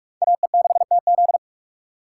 Morse code, heard as a keyed tone, about 41 wpm, 700 Hz. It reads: AE6TZ